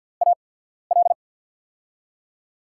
Morse code, heard as a keyed tone, about 53 wpm, spaced as for 14 wpm, 700 Hz. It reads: AL
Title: USA Call Signs - 6 Characters - 8x Character Spacing - 50wpm